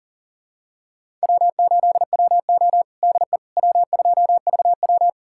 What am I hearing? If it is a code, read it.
Morse code, heard as a keyed tone, about 40 wpm, 700 Hz. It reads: W8WO DE W2VW
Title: USA Callsign DE Callsign - 4 Characters - 40wpm